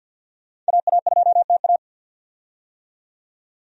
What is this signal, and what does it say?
Morse code, heard as a keyed tone, about 47 wpm, 700 Hz. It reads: AA2TA
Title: USA Call Signs - 5 Characters - 50wpm